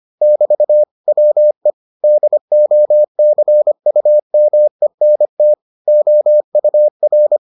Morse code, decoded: <BT> WE DOCUMENT OUR

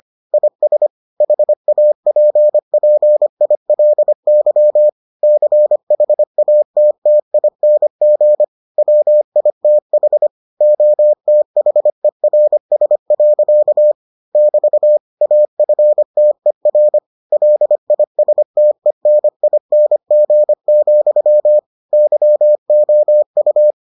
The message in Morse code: IS HAPPILY CHATTING WITH OTHERS. <BT> AFTER LISTENING, YOU

